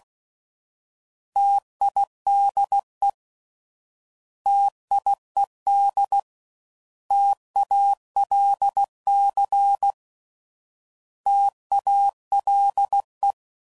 Morse code, heard as a keyed tone, about 16 wpm, 800 Hz. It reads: TIDE TIED TALC TALE
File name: Prefixes / By First Letter / Ts